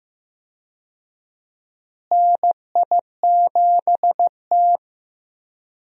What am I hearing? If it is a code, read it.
Morse code, heard as a keyed tone, about 15 wpm, 700 Hz. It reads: NI7T
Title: USA Call Signs - 4 Characters - Repeated 3x - 15wpm